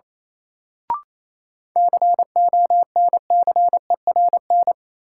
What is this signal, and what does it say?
Morse code, heard as a keyed tone, about 28 wpm, 700 Hz. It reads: CONCERN
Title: Sentences from Top 1000 Words - New Word Drill - 28wpm